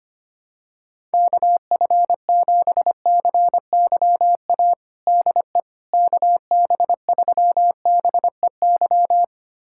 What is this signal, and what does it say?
Morse code, heard as a keyed tone, about 25 wpm, 700 Hz. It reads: KF7CYA DE KB3BEY
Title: USA Callsign DE Callsign - 6 Characters - 25wpm